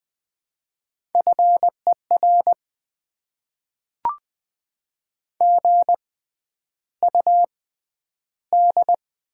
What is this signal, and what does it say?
Morse code, heard as a keyed tone, about 20 wpm, spaced as for 10 wpm, 700 Hz. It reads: FER GUD
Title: Top 20 QSO Elements - 6x Character Spacing - 20wpm